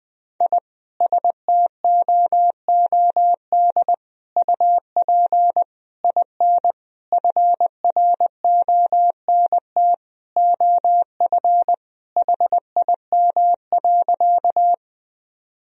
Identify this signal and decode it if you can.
Morse code, heard as a keyed tone, about 20 wpm, 700 Hz. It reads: I STOOD UP IN FRONT OF HIM.
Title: Sentences from Top 1000 Words - Review Part 2 - 20wpm